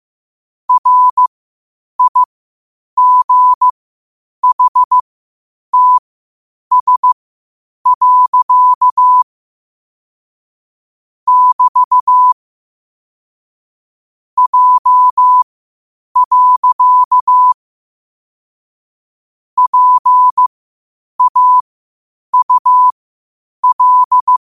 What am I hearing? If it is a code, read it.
Morse code, heard as a keyed tone, about 15 wpm, spaced as for 8 wpm, 1000 Hz. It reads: RIGHTS. <BT> J. PAUL
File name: qotdfiles / 8wpm